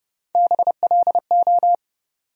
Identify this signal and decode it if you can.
Morse code, heard as a keyed tone, about 30 wpm, 700 Hz. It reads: BLO
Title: USA Call Sign Postfixes - 3 Letters 30wpm